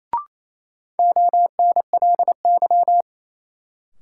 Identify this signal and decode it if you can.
Morse code, heard as a keyed tone, about 28 wpm, 700 Hz. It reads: ONLY
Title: Sentences from Top 200 Words - New Word Drill - 28wpm